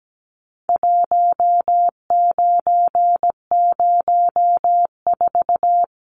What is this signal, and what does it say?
Morse code, heard as a keyed tone, about 17 wpm, 700 Hz. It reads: 1904
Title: Numbers - 4-Digit 17wpm